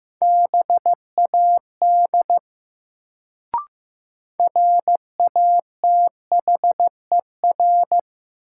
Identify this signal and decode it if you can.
Morse code, heard as a keyed tone, about 15 wpm, 700 Hz. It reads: BAD RATHER
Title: Sentences from Top 700 Words - New Word Drill - 15wpm